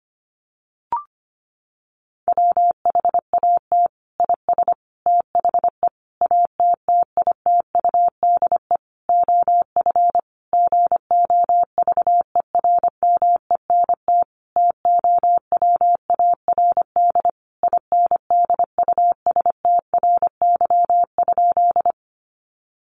Morse code, heard as a keyed tone, about 25 wpm, 700 Hz. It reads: WHAT IS THE ATTITUDE OF GOVERMENT TOWARD INDUSTRY?
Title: Sentences from Top 1300 Words 25wpm